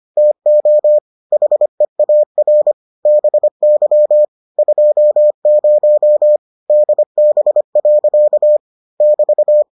Morse code, heard as a keyed tone, about 25 wpm, 600 Hz. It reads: TO HEAR BY 20 DB. <BT>